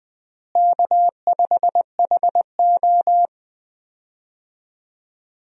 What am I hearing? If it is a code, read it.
Morse code, heard as a keyed tone, about 20 wpm, 700 Hz. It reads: K5HO
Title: USA Call Signs - 4 Characters - Repeated 3x - 20wpm